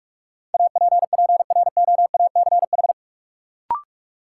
Morse code, heard as a keyed tone, about 41 wpm, 700 Hz. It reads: APPROACH
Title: Sentences from Top 1000 Words - Review Part 6 - New Word Drill - 45wpm